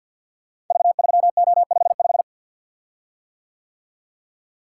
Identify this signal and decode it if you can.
Morse code, heard as a keyed tone, about 53 wpm, 700 Hz. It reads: V3O55
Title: Code Groups - 5 Letter:Numbers - 50wpm